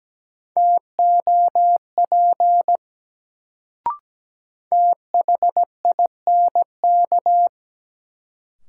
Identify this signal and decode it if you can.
Morse code, heard as a keyed tone, about 17 wpm, 700 Hz. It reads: TOP THINK
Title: Sets of 2 Words - New Word Drill - 17wpm